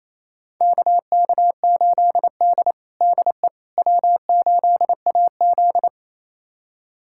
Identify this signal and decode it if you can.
Morse code, heard as a keyed tone, about 28 wpm, 700 Hz. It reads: KK8D DE W8AZ
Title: USA Callsign DE Callsign - 4 Characters - 28wpm